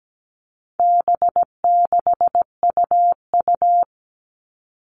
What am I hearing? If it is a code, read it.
Morse code, heard as a keyed tone, about 17 wpm, 700 Hz. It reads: B6UU
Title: All Cut Numbers - 4-Digit 17wpm